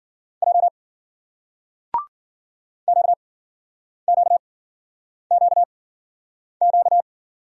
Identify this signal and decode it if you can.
Morse code, heard as a keyed tone, about 50 wpm, spaced as for 9 wpm, 700 Hz. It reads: J QQQQ
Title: Speed-Racing Single Letters 40wpm